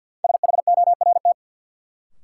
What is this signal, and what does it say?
Morse code, heard as a keyed tone, about 53 wpm, 700 Hz. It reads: SHORT
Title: Sentences from Top 400 Words - New Word Drill - 50wpm